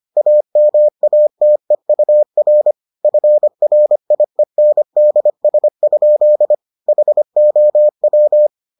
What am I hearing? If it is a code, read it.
Morse code, heard as a keyed tone, about 25 wpm, 600 Hz. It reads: AMATEUR FRIENDS? HOW